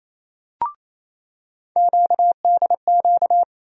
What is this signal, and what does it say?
Morse code, heard as a keyed tone, about 28 wpm, 700 Hz. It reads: QDQ
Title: USA Call Sign Postfixes - 3 Letters 28wpm